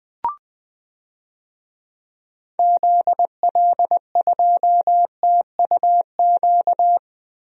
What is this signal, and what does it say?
Morse code, heard as a keyed tone, about 21 wpm, 700 Hz. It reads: ZL2TUQ
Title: DXCC-100 Call Signs - 6 Characters - Repeated 2x - 20wpm